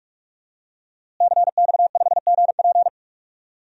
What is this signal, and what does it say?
Morse code, heard as a keyed tone, about 46 wpm, 700 Hz. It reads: CX5GP
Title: DXCC-100 Call Signs - 5 Characters - Repeated 2x - 45wpm